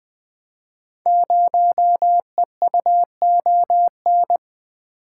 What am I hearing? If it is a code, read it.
Morse code, heard as a keyed tone, about 20 wpm, 700 Hz. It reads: 0EUON
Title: Code Groups - 5 Letter:Numbers - 20wpm